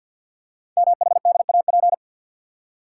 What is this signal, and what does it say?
Morse code, heard as a keyed tone, about 51 wpm, 700 Hz. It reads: MHDAP
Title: Code Groups - 5 Letters - 50wpm